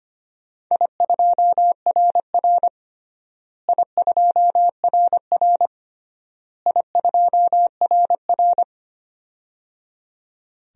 Morse code, heard as a keyed tone, about 25 wpm, 700 Hz. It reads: I2RR I2RR I2RR
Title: DXCC-100 Call Signs - 4 Characters - Repeated 3x - 25wpm